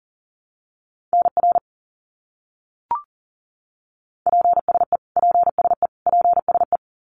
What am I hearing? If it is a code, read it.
Morse code, heard as a keyed tone, about 40 wpm, 700 Hz. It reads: NR PSE PSE PSE
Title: Abbreviations Repeated 3x - 40wpm